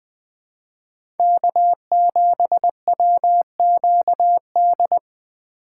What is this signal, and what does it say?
Morse code, heard as a keyed tone, about 20 wpm, 700 Hz. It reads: K7WQD
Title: USA Call Signs - 5 Characters - 3x Character Spacing - 20wpm